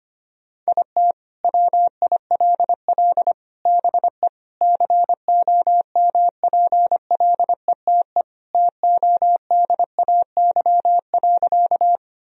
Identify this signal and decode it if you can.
Morse code, heard as a keyed tone, about 25 wpm, 700 Hz. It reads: IT WILL BE COMPLETE TODAY.
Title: Sentences from Top 1000 Words - Review Part 2 - 25wpm